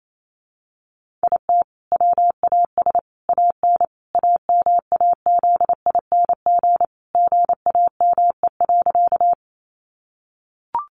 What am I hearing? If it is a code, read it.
Morse code, heard as a keyed tone, about 28 wpm, 700 Hz. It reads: IT WAS AN AMAZING GAME.
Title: Sentences from Top 1500 Words 28wpm